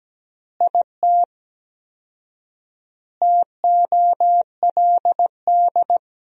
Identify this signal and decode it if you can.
Morse code, heard as a keyed tone, about 17 wpm, 700 Hz. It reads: IT TOLD
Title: 4x Word-Spacing - Sentences from Top 400 Words - 17wpm